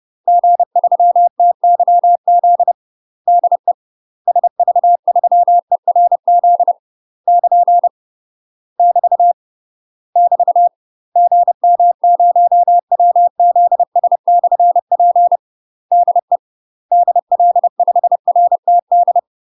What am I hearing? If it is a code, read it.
Morse code, heard as a keyed tone, about 30 wpm, 700 Hz. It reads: G3TYZ DE SV3ERZ <KN> <BT> <BT> GM0WZS/P DE DL5RTD
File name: UK MASTER QSOs TRAINING / 30WPM / UK MASTER QSOs 0031